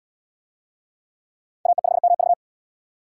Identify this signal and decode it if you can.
Morse code, heard as a keyed tone, about 68 wpm, 700 Hz. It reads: AE5MV